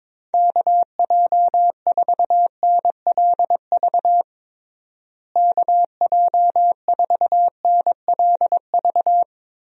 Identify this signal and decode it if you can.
Morse code, heard as a keyed tone, about 21 wpm, 700 Hz. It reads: KJ4NLV KJ4NLV
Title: USA Call Signs - 6 Characters - Repeated 2x - 22wpm